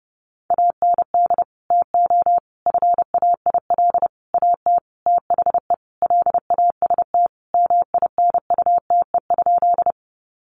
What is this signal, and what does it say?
Morse code, heard as a keyed tone, about 30 wpm, 700 Hz. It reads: AND TO FAIL AT THE LAST MINUTE?
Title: Sentences from Top 1100 Words 30wpm